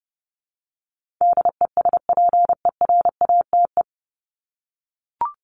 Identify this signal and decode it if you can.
Morse code, heard as a keyed tone, about 30 wpm, 700 Hz. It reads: DESPERATE
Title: Top 1751-2000 Words 30wpm